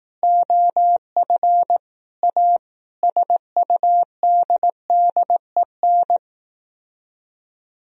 Morse code, heard as a keyed tone, about 18 wpm, 700 Hz. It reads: OF A SUDDEN
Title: Speed-Racing - Sets of 5 Words 15wpm